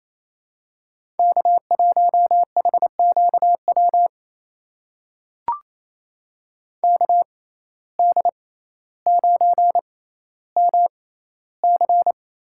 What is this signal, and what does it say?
Morse code, heard as a keyed tone, about 28 wpm, spaced as for 14 wpm, 700 Hz. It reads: K1HQW KD9MC
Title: USA Call Signs - 5 Characters - 6x Character Spacing - 28wpm